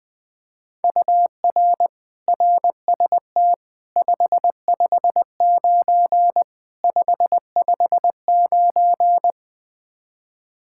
Morse code, heard as a keyed tone, about 21 wpm, 700 Hz. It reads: UR RST 559 559
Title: K1USN SST QSO Practice 20wpm